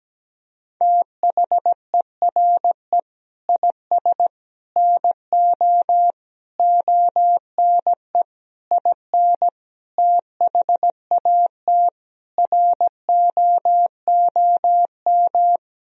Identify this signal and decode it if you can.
Morse code, heard as a keyed tone, about 17 wpm, 700 Hz. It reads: THERE IS NO ONE IN THAT ROOM
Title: Sentences from Top 500 Words and Abbreviations 17wpm